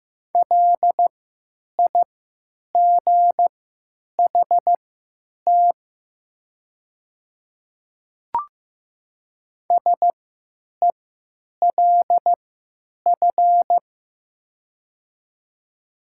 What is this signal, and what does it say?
Morse code, heard as a keyed tone, about 15 wpm, spaced as for 9 wpm, 700 Hz. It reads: LIGHT SELF
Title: 3x Character Spacing - Top 200 Words - 15wpm